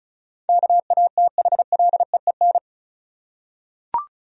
Morse code, heard as a keyed tone, about 36 wpm, 700 Hz. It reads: KATHLEEN
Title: USA Common Names - 35wpm